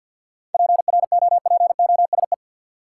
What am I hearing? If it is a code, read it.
Morse code, heard as a keyed tone, about 53 wpm, 700 Hz. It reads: PROPOSE
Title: Seven Letter Words - 50wpm